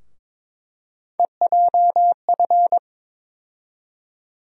Morse code, heard as a keyed tone, about 22 wpm, 700 Hz. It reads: EJF
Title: USA Call Sign Postfixes - 3 Letters 22wpm